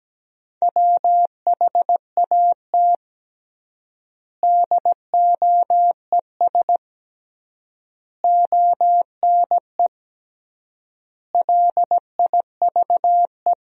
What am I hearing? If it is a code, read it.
Morse code, heard as a keyed tone, about 17 wpm, 700 Hz. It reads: WHAT DOES ONE LIVE
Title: 3x Word-Spacing - Sentences from Top 200 Words - 17wpm